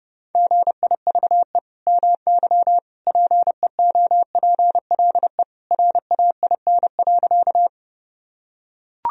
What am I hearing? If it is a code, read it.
Morse code, heard as a keyed tone, about 32 wpm, 700 Hz. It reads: GIVE MY PEOPLE RAIN.
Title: Sentences from Top 1000 Words - Review Part 1 - 30wpm